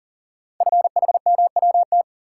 Morse code, heard as a keyed tone, about 40 wpm, 700 Hz. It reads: FHMWT
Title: Code Groups - 5 Letters - 40wpm